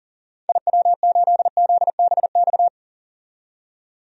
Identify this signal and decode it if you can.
Morse code, heard as a keyed tone, about 40 wpm, 700 Hz. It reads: IW8ZBX
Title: DXCC-100 Call Signs - 6 Characters - 40wpm